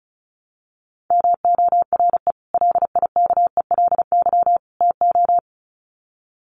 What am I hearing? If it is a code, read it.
Morse code, heard as a keyed tone, about 34 wpm, 700 Hz. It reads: MORE LIKELY TO
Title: Sets of 3 Words 35wpm